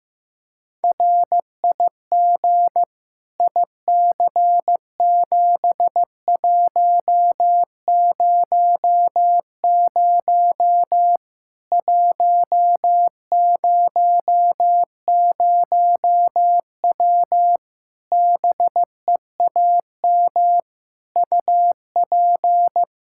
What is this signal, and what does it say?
Morse code, heard as a keyed tone, about 15 wpm, 700 Hz. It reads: RIG IC7100 100W BEAM UP
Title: Practice QSO - Exchange 2 - Info - 15wpm